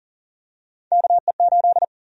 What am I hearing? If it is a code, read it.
Morse code, heard as a keyed tone, about 40 wpm, 700 Hz. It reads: KE8
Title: USA Call Sign Prefixes - 40wpm